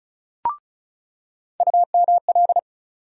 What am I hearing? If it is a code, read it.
Morse code, heard as a keyed tone, about 36 wpm, 700 Hz. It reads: UML
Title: USA Call Sign Postfixes - 3 Letters 35wpm